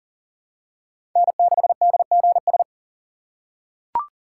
Morse code, heard as a keyed tone, about 39 wpm, 700 Hz. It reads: N6DGS
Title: USA Call Signs - 5 Characters - 3x Character Spacing - 40wpm